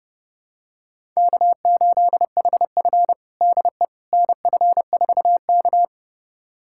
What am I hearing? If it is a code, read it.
Morse code, heard as a keyed tone, about 30 wpm, 700 Hz. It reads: K8HF DE NF4K
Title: USA Callsign DE Callsign - 4 Characters - 30wpm